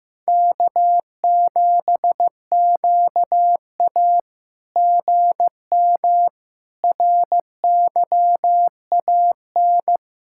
Morse code, heard as a keyed tone, about 15 wpm, 700 Hz. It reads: K7QA GM RYAN
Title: SOTA QSO Practice 15wpm